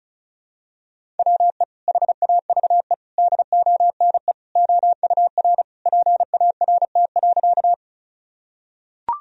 Morse code, heard as a keyed tone, about 34 wpm, 700 Hz. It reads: WE HAVE DONE OUR PART.
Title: Sentences from Top 1000 Words - Review Part 1 - 35wpm